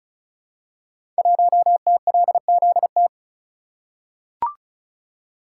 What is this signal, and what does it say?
Morse code, heard as a keyed tone, about 36 wpm, 700 Hz. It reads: 1TLZT
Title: Code Groups - 5 Letter:Numbers - 35wpm